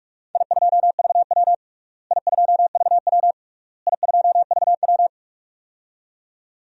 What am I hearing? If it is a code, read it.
Morse code, heard as a keyed tone, about 43 wpm, 700 Hz. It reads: I2VW I2VW I2VW
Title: DXCC-100 Call Signs - 4 Characters - Repeated 3x - 45wpm